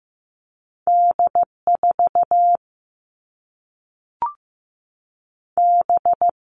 Morse code, heard as a keyed tone, about 15 wpm, 700 Hz. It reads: D 4 B
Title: All Cut Numbers - 2-Digit 15wpm